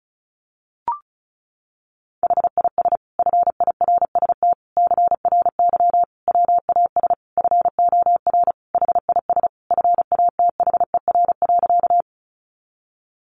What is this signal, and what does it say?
Morse code, heard as a keyed tone, about 34 wpm, 700 Hz. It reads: HIS FIRST CRY WAS FOR HIS FATHER.
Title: Sentences from Top 500 Words 35wpm